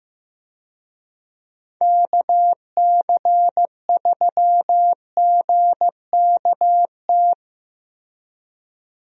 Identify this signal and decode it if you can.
Morse code, heard as a keyed tone, about 15 wpm, 700 Hz. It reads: KC3GKT
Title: USA Call Signs - 6 Characters - Repeated 3x - 15wpm